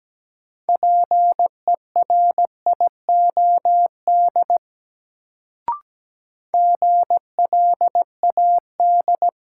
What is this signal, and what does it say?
Morse code, heard as a keyed tone, about 17 wpm, 700 Hz. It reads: PERIOD GLAD
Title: Sentences from Top 800 Words - New Word Drill - 17wpm